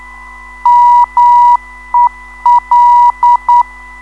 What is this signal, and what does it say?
Morse code, heard as a keyed tone, about 9 wpm, 960 Hz. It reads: MEL
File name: ndb zvok / mel-ron-mel